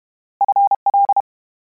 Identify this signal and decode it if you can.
Morse code, heard as a keyed tone, about 32 wpm, 800 Hz. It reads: FL
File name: Combinations / Combinations-mp3s / Combinations@30